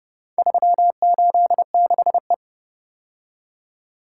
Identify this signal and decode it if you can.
Morse code, heard as a keyed tone, about 30 wpm, 700 Hz. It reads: 386E